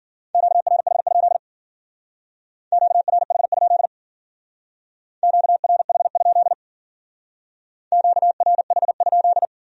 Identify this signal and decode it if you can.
Morse code, heard as a keyed tone, about 51 wpm, 700 Hz. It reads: QRH? QRH? QRH? QRH?